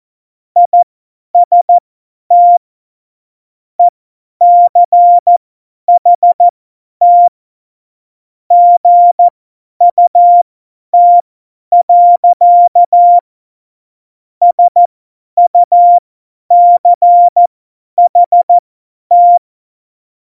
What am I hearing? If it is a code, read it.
Morse code, heard as a keyed tone, about 14 wpm, spaced as for 10 wpm, 700 Hz. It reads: IST ECHT GUT. SUCHT
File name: Morsekurs / Witze / Witz 27. Februar 2018, 50 F70